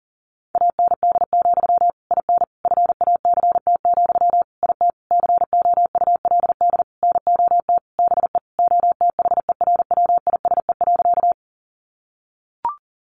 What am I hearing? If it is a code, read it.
Morse code, heard as a keyed tone, about 40 wpm, 700 Hz. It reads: AND, IN FACT, IT COULD NOT BE OTHERWISE.
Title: Sentences from Top 1200 Words 40wpm